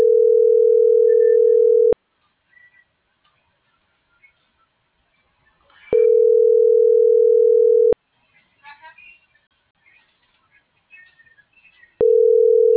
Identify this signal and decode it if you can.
Ringback tone (North America), 440 + 480 Hz.